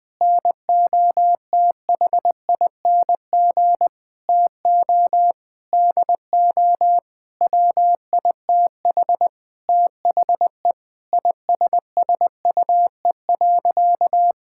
Morse code, heard as a keyed tone, about 21 wpm, 700 Hz. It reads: NOTHING TO DO WITH THE ISSUE.
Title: Sentences with 6-Word Phrases - 20wpm